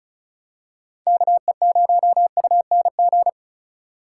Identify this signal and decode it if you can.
Morse code, heard as a keyed tone, about 36 wpm, 700 Hz. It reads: KE0UNG
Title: USA Call Signs - 6 Characters - 2x Character Spacing - 35wpm